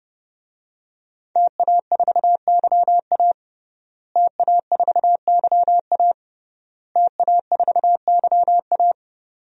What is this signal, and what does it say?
Morse code, heard as a keyed tone, about 30 wpm, 700 Hz. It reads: TA4YA TA4YA TA4YA
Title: DXCC-100 Call Signs - 5 Characters - Repeated 3x - 30wpm